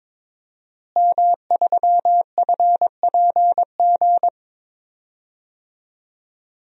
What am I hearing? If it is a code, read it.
Morse code, heard as a keyed tone, about 22 wpm, 700 Hz. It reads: M3FPG
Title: DXCC-100 Call Signs - 5 Characters - Repeated 3x - 22wpm